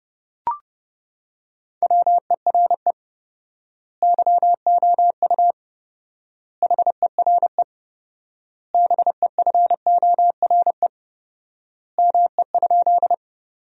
Morse code, heard as a keyed tone, about 31 wpm, 700 Hz. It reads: WERE YOU HERE BEFORE ME?
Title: 4x Word-Spacing - Sentences from Top 200 Words - 30wpm